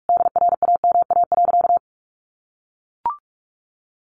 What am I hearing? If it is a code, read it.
Morse code, heard as a keyed tone, about 44 wpm, 700 Hz. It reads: DRAMA.
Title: Sentences from Top 1500 Words 45wpm